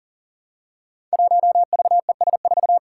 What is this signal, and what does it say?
Morse code, heard as a keyed tone, about 38 wpm, 700 Hz. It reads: 1VES4
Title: Five Challenging Letter-Number Combinations 40wpm